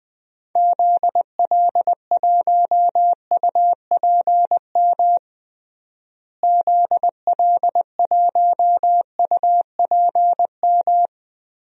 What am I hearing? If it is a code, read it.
Morse code, heard as a keyed tone, about 20 wpm, 700 Hz. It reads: ZL1UPM ZL1UPM
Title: DXCC-100 Call Signs - 6 Characters - Repeated 3x - 20wpm